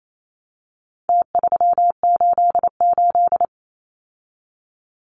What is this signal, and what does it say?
Morse code, heard as a keyed tone, about 28 wpm, 700 Hz. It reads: T388
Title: Numbers - Contest Style - 4-Digit 28wpm